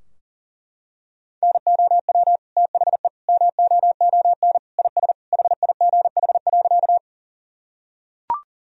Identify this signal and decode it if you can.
Morse code, heard as a keyed tone, about 37 wpm, 700 Hz. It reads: NOW THE MOON IS HIGH.
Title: Sentences from Top 1000 Words - Review Part 3 - 40wpm